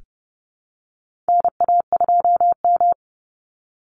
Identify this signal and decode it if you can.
Morse code, heard as a keyed tone, about 31 wpm, 700 Hz. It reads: NA2M
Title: Call Signs - 30wpm